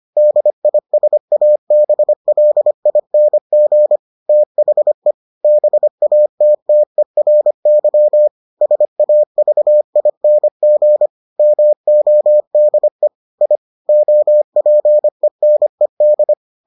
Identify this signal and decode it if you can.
Morse code, heard as a keyed tone, about 25 wpm, 600 Hz. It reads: DISABLING THE BATTERY SAVING MODE I OPENED